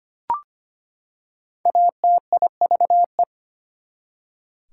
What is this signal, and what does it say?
Morse code, heard as a keyed tone, about 25 wpm, 700 Hz. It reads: ATIVE
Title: Prefixes and Suffixes 25wpm